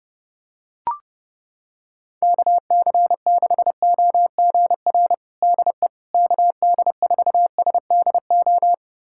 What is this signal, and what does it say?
Morse code, heard as a keyed tone, about 32 wpm, 700 Hz. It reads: KC6OGR DE KD4SDO
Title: USA Callsign DE Callsign 30wpm